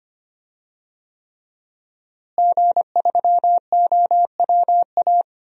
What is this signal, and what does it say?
Morse code, heard as a keyed tone, about 25 wpm, 700 Hz. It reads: G3OWA
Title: DXCC-100 Call Signs - 5 Characters - Repeated 3x - 25wpm